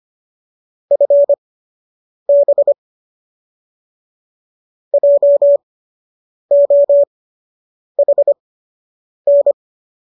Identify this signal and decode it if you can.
Morse code, heard as a keyed tone, about 25 wpm, spaced as for 8 wpm, 570 Hz. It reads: FB JOHN